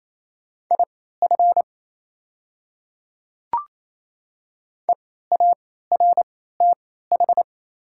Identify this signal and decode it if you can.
Morse code, heard as a keyed tone, about 28 wpm, spaced as for 16 wpm, 700 Hz. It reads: IF EARTH
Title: 3x Character Spacing - Top 200 Words - 28wpm